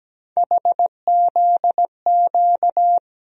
Morse code, heard as a keyed tone, about 17 wpm, 700 Hz. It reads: HZQ
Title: USA Call Sign Postfixes - 3 Letters 17wpm